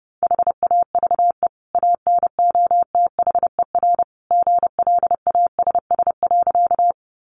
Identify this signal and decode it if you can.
Morse code, heard as a keyed tone, about 29 wpm, 700 Hz. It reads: HAVE ANOTHER GLASS.
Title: Sentences from Top 1300 Words 30wpm